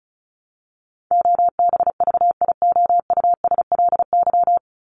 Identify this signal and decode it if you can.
Morse code, heard as a keyed tone, about 34 wpm, 700 Hz. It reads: OBVIOUSLY
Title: LY Suffix - Top 100 Words with Base Words 35wpm